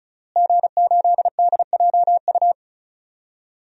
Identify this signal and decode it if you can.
Morse code, heard as a keyed tone, about 36 wpm, 700 Hz. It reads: G8DJU
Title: DXCC-100 Call Signs - 5 Characters - Repeated 3x - 35wpm